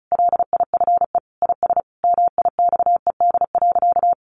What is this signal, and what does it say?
Morse code, heard as a keyed tone, about 34 wpm, 700 Hz. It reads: LIFE IS MIXED.
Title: Sentences from Top 1500 Words 35wpm